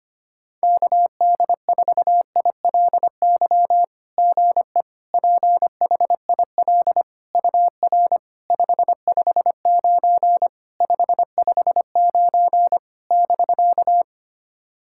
Morse code, heard as a keyed tone, about 25 wpm, 700 Hz. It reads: KD4ILY GE PHIL UR 559 559 <BK>
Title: SOTA QSO Practice 25wpm